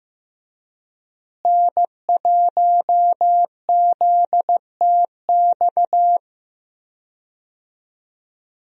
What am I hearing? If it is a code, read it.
Morse code, heard as a keyed tone, about 15 wpm, 700 Hz. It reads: N1ZTX
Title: USA Call Signs - 5 Characters - Repeated 3x - 15wpm